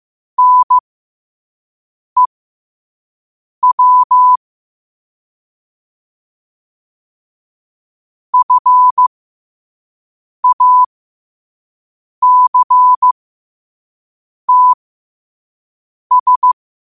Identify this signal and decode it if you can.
Morse code, heard as a keyed tone, about 15 wpm, spaced as for 5 wpm, 1000 Hz. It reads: NEW FACTS
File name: qotdfiles / 5wpm